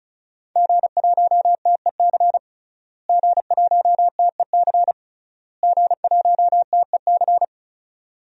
Morse code, heard as a keyed tone, about 36 wpm, 700 Hz. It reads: G1TEC G1TEC G1TEC
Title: DXCC-100 Call Signs - 5 Characters - Repeated 3x - 35wpm